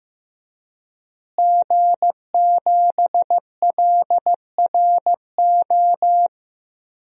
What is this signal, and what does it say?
Morse code, heard as a keyed tone, about 15 wpm, 700 Hz. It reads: G7LRO
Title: DXCC-100 Call Signs - 5 Characters - Repeated 3x - 15wpm